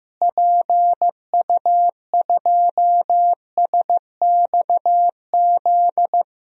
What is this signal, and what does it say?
Morse code, heard as a keyed tone, about 16 wpm, 700 Hz. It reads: PU2SXZ
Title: DXCC-100 Call Signs - 6 Characters - Repeated 2x - 15wpm